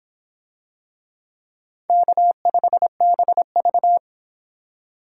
Morse code, heard as a keyed tone, about 24 wpm, 700 Hz. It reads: K5BV